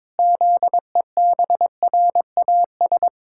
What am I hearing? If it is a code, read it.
Morse code, heard as a keyed tone, about 22 wpm, 700 Hz. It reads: ZEBRAS
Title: Pangram Challenge v2 - 22wpm